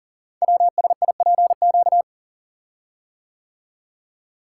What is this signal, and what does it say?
Morse code, heard as a keyed tone, about 39 wpm, 700 Hz. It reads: WSIPQ
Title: Code Groups - 5 Letters - 40wpm